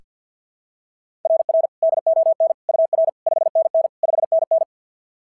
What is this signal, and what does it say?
Morse code, heard as a keyed tone, about 47 wpm, 690 Hz. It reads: RR DON UR 5NN 5NN
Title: SOTA QSO Practice 50wpm